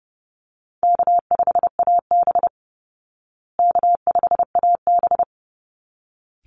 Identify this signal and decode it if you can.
Morse code, heard as a keyed tone, about 30 wpm, 700 Hz. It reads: K5AB K5AB
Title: Call Signs Repeated 3x - 30wpm